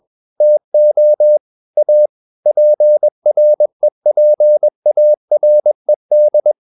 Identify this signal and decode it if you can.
Morse code, heard as a keyed tone, about 21 wpm, 600 Hz. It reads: TO A PREPARED
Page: Romanen Never af Ken Follet på engelsk.